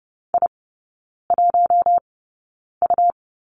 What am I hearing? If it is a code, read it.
Morse code, heard as a keyed tone, about 30 wpm, spaced as for 9 wpm, 700 Hz. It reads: I1U
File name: Lesson 17e - Letter u - Three Characters 30wpm